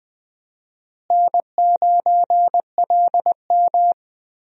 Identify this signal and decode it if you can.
Morse code, heard as a keyed tone, about 21 wpm, 700 Hz. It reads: N9LM
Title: USA Call Signs - 4 Characters - Repeated 3x - 20wpm